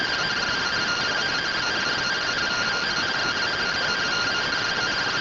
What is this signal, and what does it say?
Two-tone teleprinter signal, about 170 Hz shift.